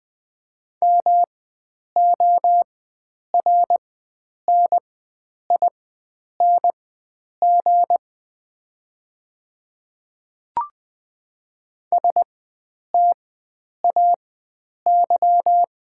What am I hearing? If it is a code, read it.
Morse code, heard as a keyed tone, about 20 wpm, spaced as for 9 wpm, 700 Hz. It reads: MORNING STAY
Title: 4x Character Spacing - Top 500 Words - 20wpm